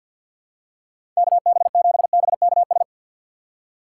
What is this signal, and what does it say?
Morse code, heard as a keyed tone, about 54 wpm, 700 Hz. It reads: KB7BKS
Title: USA Call Signs - 6 Characters - 8x Character Spacing - 50wpm